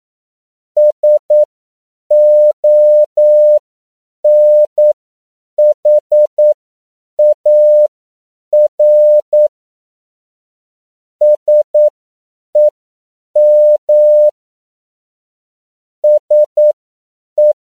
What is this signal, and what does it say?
Morse code, heard as a keyed tone, about 9 wpm, spaced as for 7 wpm, 600 Hz. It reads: SONHAR SEM SE